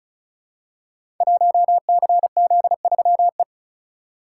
Morse code, heard as a keyed tone, about 36 wpm, 690 Hz. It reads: 1CZ3E
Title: Code Groups - 5 Letter:Numbers - 35wpm